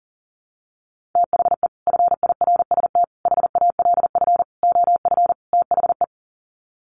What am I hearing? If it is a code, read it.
Morse code, heard as a keyed tone, about 40 wpm, 700 Hz. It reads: THE FIRST HALF OF THE
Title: Sets of 5 Words 40wpm